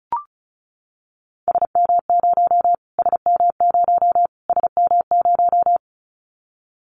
Morse code, heard as a keyed tone, about 34 wpm, 700 Hz. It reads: SM0 SM0 SM0
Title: Callsign Prefixes for Top 25 DXCC - Repeated 3x - 35wpm